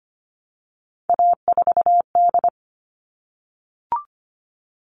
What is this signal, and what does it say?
Morse code, heard as a keyed tone, about 25 wpm, 700 Hz. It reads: A4D
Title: All Cut Numbers - 3-Digit 25wpm